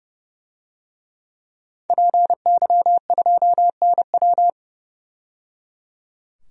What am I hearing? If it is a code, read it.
Morse code, heard as a keyed tone, about 30 wpm, 700 Hz. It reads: PY2NW
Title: DXCC-100 Call Signs - 5 Characters - 30wpm